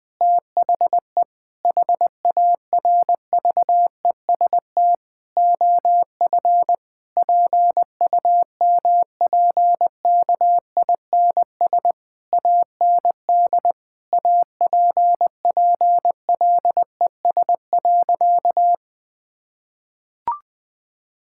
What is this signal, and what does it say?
Morse code, heard as a keyed tone, about 20 wpm, 700 Hz. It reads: THE HARVEST OF PUMPKINS AND APPLES.
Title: Difficult Sentences with Hints - 20wpm